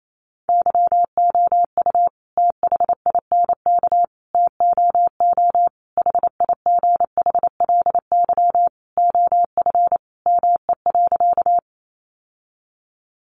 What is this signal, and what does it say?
Morse code, heard as a keyed tone, about 28 wpm, 700 Hz. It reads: YOU THINK TOO HIGHLY OF ME.
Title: Sentences from Top 1100 Words 28wpm